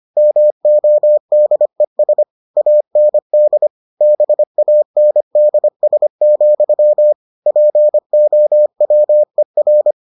Morse code, heard as a keyed tone, about 25 wpm, 600 Hz. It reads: MODES AND BANDS, POWER